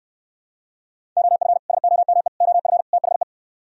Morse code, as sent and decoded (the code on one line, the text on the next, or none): --- ..-. / ... --- -- . / --- ..-. / - .... .
OF SOME OF THE